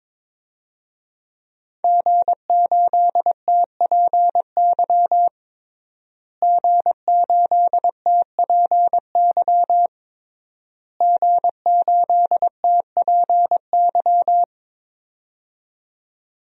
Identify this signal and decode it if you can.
Morse code, heard as a keyed tone, about 22 wpm, 700 Hz. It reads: G8TPY G8TPY G8TPY
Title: DXCC-100 Call Signs - 5 Characters - Repeated 3x - 22wpm